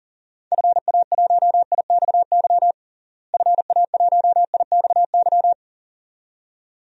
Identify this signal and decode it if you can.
Morse code, heard as a keyed tone, about 40 wpm, 700 Hz. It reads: FA1IXY FA1IXY
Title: DXCC-100 Call Signs - 6 Characters - Repeated 2x - 40wpm